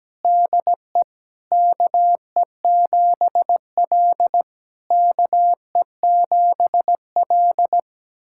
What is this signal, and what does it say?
Morse code, heard as a keyed tone, about 17 wpm, 700 Hz. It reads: DE KE7L KE7L
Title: SOTA QSO Practice 17wpm